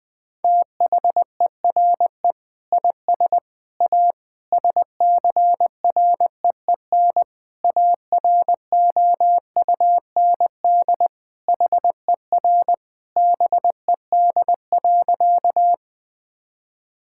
Morse code, heard as a keyed tone, about 20 wpm, 700 Hz. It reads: THERE IS A SCREEN AROUND HER BED.
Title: Sentences from Top 1000 Words - Review Part 6 - 20wpm